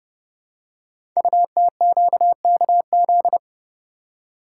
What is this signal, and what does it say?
Morse code, heard as a keyed tone, about 28 wpm, 700 Hz. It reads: UTQKZ
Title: Code Groups - 5 Letters - 30wpm